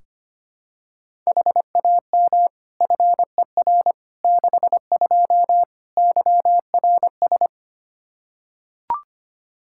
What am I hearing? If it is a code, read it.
Morse code, heard as a keyed tone, about 25 wpm, 700 Hz. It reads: HAM FER 62 YRS
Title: Practice QSO - Exchange 3 - More Info - 25wpm